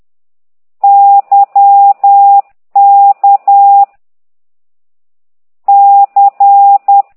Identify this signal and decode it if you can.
Morse code, heard as a keyed tone, about 10 wpm, 800 Hz. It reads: Y K C
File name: exerc7